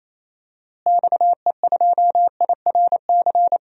Morse code, heard as a keyed tone, about 28 wpm, 700 Hz. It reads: XE2IRC
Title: DXCC-100 Call Signs - 6 Characters - Repeated 3x - 28wpm